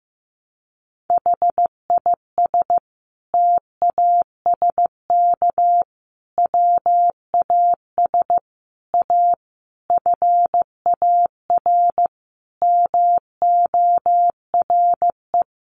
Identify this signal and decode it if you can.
Morse code, heard as a keyed tone, about 15 wpm, 700 Hz. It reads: HIS TASK WAS A FAR MORE
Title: Sentences from Top 1200 Words 15wpm